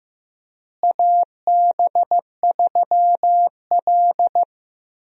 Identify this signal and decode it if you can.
Morse code, heard as a keyed tone, about 15 wpm, 700 Hz. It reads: AB3L
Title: USA Call Signs - 4 Characters - Repeated 2x - 15wpm